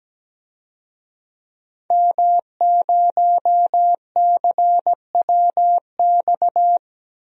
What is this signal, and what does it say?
Morse code, heard as a keyed tone, about 17 wpm, 700 Hz. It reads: M0CWX
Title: DXCC-100 Call Signs - 5 Characters - Repeated 3x - 17wpm